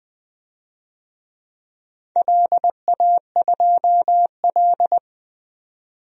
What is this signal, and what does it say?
Morse code, heard as a keyed tone, about 21 wpm, 700 Hz. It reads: LA2L
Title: DXCC-100 Call Signs - 4 Characters - 20wpm